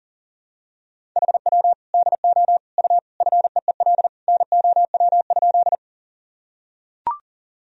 Morse code, heard as a keyed tone, about 40 wpm, 700 Hz. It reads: HW DO U FEEL NOW?
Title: Sentences from Top 400 Words and Abbreviations 40wpm